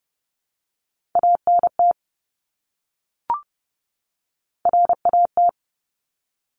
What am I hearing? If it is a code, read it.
Morse code, heard as a keyed tone, about 30 wpm, 700 Hz. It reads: ANT RAT
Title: Top Three Letter Combinations in Words 30wpm